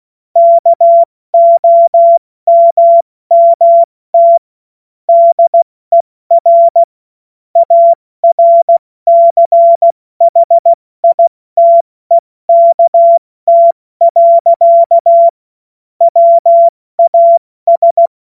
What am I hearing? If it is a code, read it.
Morse code, heard as a keyed tone, about 16 wpm, spaced as for 14 wpm, 680 Hz. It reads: KOMMT DER ARCHITEKT. WAS
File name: Morsekurs / Witze / Witz 20. Januar 2026 70 F80